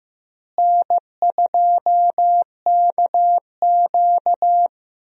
Morse code, heard as a keyed tone, about 15 wpm, 700 Hz. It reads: N2KQ
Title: USA Call Signs - 4 Characters - Repeated 2x - 15wpm